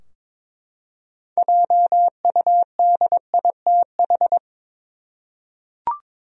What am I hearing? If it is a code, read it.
Morse code, heard as a keyed tone, about 22 wpm, 700 Hz. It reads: JUDITH
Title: USA Common Names - 22wpm